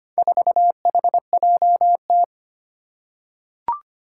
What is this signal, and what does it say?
Morse code, heard as a keyed tone, about 25 wpm, 700 Hz. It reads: 4HJT
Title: Four Challenging Letter-Number Combinations 25wpm